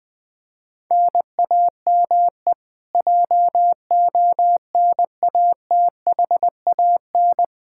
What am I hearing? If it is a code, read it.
Morse code, heard as a keyed tone, about 20 wpm, 700 Hz. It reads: NAME JONATHAN
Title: Practice QSO - Exchange 1 - Beginning - 20wpm